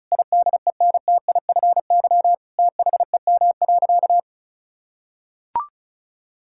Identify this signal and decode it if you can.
Morse code, heard as a keyed tone, about 36 wpm, 700 Hz. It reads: IDENTIFY THEM.
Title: Sentences from Top 1000 Words - Review Part 6 - 35wpm